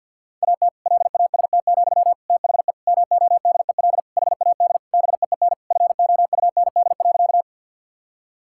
Morse code, heard as a keyed tone, about 51 wpm, 700 Hz. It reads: AT LAST, THE MODEL HAD BEEN FOUND.
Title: Sentences from Top 1000 Words - Review Part 5 - 50wpm